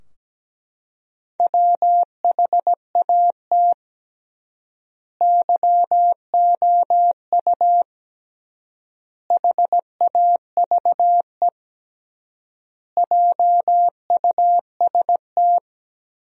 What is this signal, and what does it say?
Morse code, heard as a keyed tone, about 17 wpm, 700 Hz. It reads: WHAT YOU HAVE JUST
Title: 3x Word-Spacing - Sentences from Top 200 Words - 17wpm